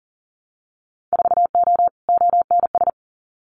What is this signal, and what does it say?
Morse code, heard as a keyed tone, about 41 wpm, 700 Hz. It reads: 4O ONS
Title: Winter Field Day Exchanges 40wpm